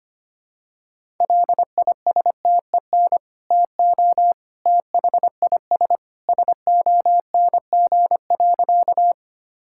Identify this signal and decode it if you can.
Morse code, heard as a keyed tone, about 25 wpm, 700 Hz. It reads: LISTEN TO THIS SONG.
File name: practice / 25wpm / Sentences with 2-Word Phrases - 25wpm